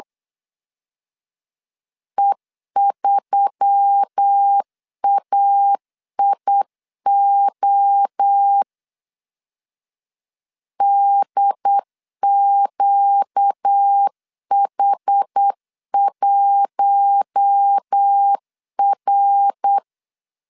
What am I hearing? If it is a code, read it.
Morse code, heard as a keyed tone, about 8 wpm, 800 Hz. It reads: E3AIO DQH1R
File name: LEZIONI / 6 PKX0 / 6 PK 8wpm 3003